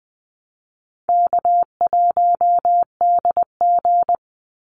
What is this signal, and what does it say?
Morse code, heard as a keyed tone, about 20 wpm, 700 Hz. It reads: K1DG
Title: Easy Call Signs Repeated 3x - 20wpm